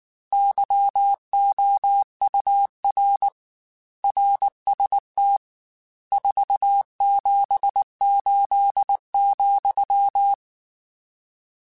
Morse code, heard as a keyed tone, about 19 wpm, 800 Hz. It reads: YOUR RST 478,
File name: RSTs / RST-mp3s / Your RSTs @18